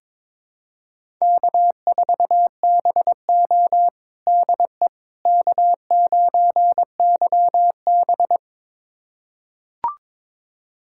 Morse code, heard as a keyed tone, about 22 wpm, 700 Hz. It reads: K4BO DE K9YB
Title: USA Callsign DE Callsign - 4 Characters - 22wpm